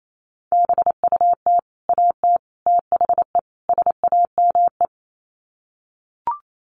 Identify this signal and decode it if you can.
Morse code, heard as a keyed tone, about 28 wpm, 700 Hz. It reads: BUT AT THE SAME
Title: Sets of 4 Words 28wpm